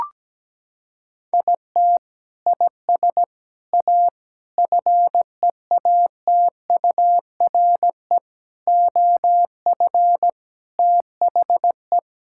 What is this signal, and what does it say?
Morse code, heard as a keyed tone, about 17 wpm, 700 Hz. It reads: IT IS A FEATURE OF THE
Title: Sentences from Top 1600 Words 17wpm